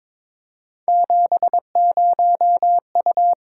Morse code, heard as a keyed tone, about 22 wpm, 700 Hz. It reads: 70U
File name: Three Letter-Number Combinations - 22wpm